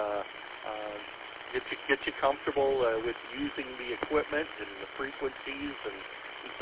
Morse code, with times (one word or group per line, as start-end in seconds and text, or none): none